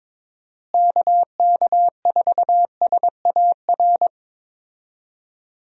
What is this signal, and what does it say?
Morse code, heard as a keyed tone, about 22 wpm, 700 Hz. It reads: KK4SAR
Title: USA Call Signs - 6 Characters - Repeated 3x - 22wpm